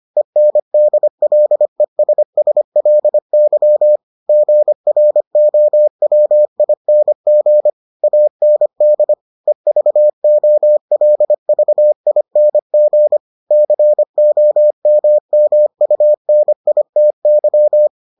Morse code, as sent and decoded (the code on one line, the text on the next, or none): . -. -.. .-.. ... ... .-.. -.-- / --. .-. --- .-- .. -. --. / .- -. -.. / . ...- --- .-.. ...- .. -. --. / -.-. --- -- -- ..- -. .. - -.--
ENDLESSLY GROWING AND EVOLVING COMMUNITY